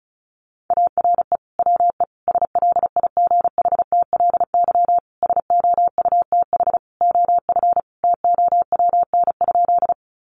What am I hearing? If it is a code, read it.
Morse code, heard as a keyed tone, about 34 wpm, 700 Hz. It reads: ARE WE SLIGHTLY SOUTH OF TOWN?
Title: Sentences from Top 1300 Words 35wpm